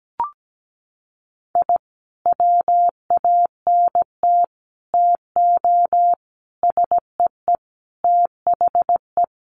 Morse code, heard as a keyed tone, about 17 wpm, 700 Hz. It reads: I WANT TO SEE THE
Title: Sentences from Top 1300 Words 17wpm